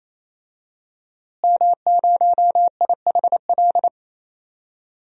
Morse code, heard as a keyed tone, about 28 wpm, 700 Hz. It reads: M0IHL
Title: DXCC-100 Call Signs - 5 Characters - Repeated 3x - 28wpm